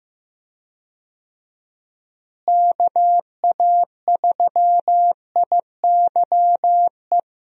Morse code, heard as a keyed tone, about 16 wpm, 700 Hz. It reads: KA3IYE
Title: USA Call Signs - 6 Characters - Repeated 2x - 15wpm